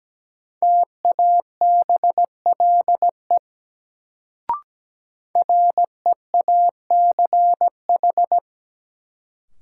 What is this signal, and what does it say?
Morse code, heard as a keyed tone, about 17 wpm, 700 Hz. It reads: TABLE REACH
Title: Sentences from Top 400 Words - New Word Drill - 17wpm